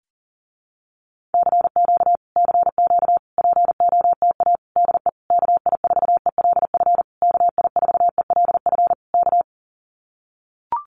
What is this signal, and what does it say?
Morse code, heard as a keyed tone, about 40 wpm, 700 Hz. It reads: CQ CQ POTA DE KI4ELF KI4ELF K
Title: Common QSO Phrases Encore 40wpm